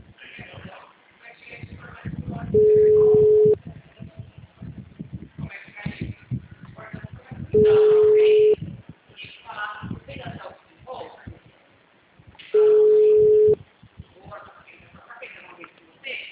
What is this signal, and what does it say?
Ringback tone (Europe, 425 Hz), with speech.
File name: out- -2018- - - .12939